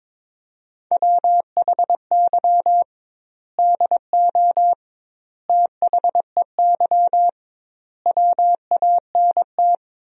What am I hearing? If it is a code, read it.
Morse code, heard as a keyed tone, about 22 wpm, 700 Hz. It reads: WHY DO THEY WANT
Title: 2x Word-Spacing - Sentences from Top 400 Words - 22wpm